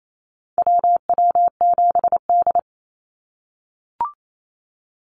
Morse code, heard as a keyed tone, about 28 wpm, 700 Hz. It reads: WW7D
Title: NA SOTA Activator Callsigns 28wpm